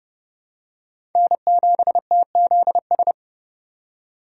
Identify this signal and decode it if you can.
Morse code, heard as a keyed tone, about 32 wpm, 700 Hz. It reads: N7TZS
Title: USA Call Signs - 5 Characters - 3x Character Spacing - 30wpm